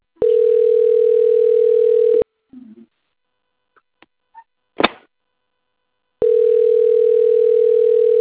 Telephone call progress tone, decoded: ringback tone

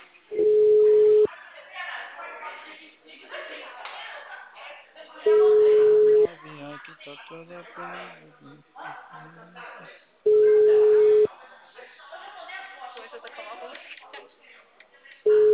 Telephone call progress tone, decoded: ringback tone

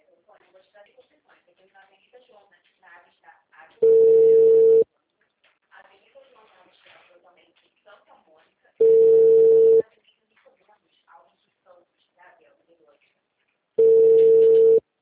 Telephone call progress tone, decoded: ringback tone

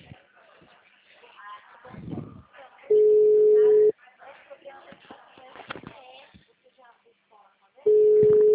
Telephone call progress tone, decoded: ringback tone

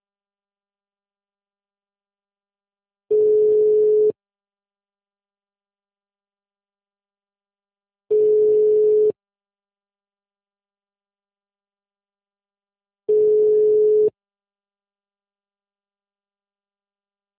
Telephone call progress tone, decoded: ringback tone

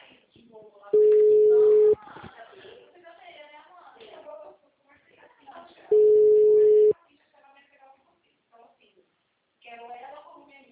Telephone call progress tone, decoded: ringback tone

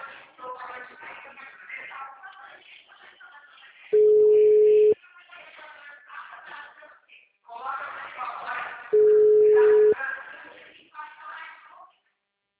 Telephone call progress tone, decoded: ringback tone

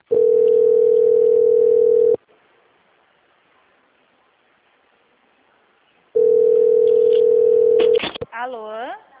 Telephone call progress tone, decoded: ringback tone

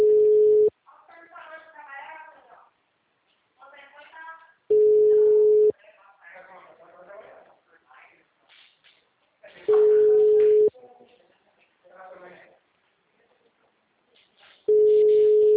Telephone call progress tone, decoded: ringback tone

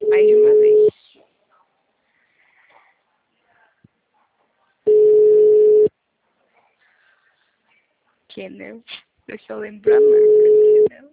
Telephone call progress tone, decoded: ringback tone

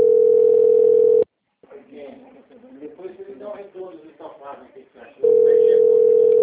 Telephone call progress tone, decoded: ringback tone

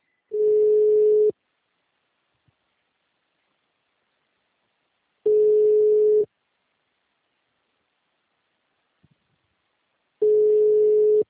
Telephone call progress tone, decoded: ringback tone

